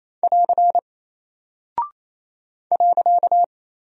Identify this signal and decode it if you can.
Morse code, heard as a keyed tone, about 28 wpm, 700 Hz. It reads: <AR> .
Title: Single Character (Letter:Number:Punctuation:Prosign) 28wpm